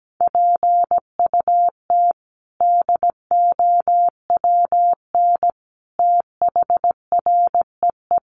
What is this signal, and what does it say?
Morse code, heard as a keyed tone, about 17 wpm, 700 Hz. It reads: PUT DOWN THREE
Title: Sentences from Top 400 Words 17wpm